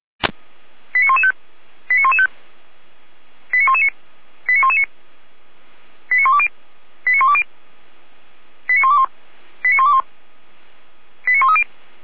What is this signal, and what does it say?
Five-tone selective call (ZVEI), heard as quick runs of five tones, tones of about 70 ms. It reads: 89115 89115 89119 89119 89120 89120 89121 89121 89130